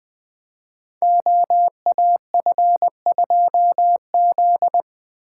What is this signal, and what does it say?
Morse code, heard as a keyed tone, about 20 wpm, 700 Hz. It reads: OAF2Z
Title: Code Groups - 5 Letter:Numbers - 20wpm